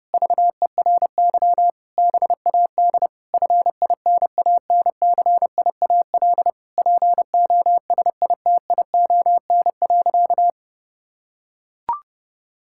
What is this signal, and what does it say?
Morse code, heard as a keyed tone, about 30 wpm, 700 Hz. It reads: VERY BAD FINANCIAL POSITION.
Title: Sentences from Top 1000 Words - Review Part 5 - 30wpm